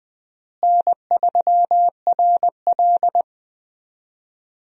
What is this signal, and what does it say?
Morse code, heard as a keyed tone, about 21 wpm, 700 Hz. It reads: N3RL
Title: USA Call Signs - 4 Characters - Repeated 3x - 20wpm